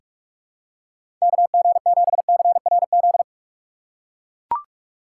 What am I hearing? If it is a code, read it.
Morse code, heard as a keyed tone, about 43 wpm, 700 Hz. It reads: KG7CRZ
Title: USA Call Signs - 6 Characters - 6x Character Spacing - 45wpm